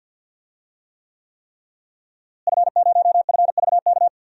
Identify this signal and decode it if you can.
Morse code, heard as a keyed tone, about 52 wpm, 700 Hz. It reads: F0FVK
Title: DXCC-100 Call Signs - 5 Characters - Repeated 3x - 50wpm